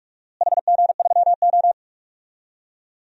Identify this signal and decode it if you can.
Morse code, heard as a keyed tone, about 46 wpm, 700 Hz. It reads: HG3O
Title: DXCC-100 Call Signs - 4 Characters - Repeated 3x - 45wpm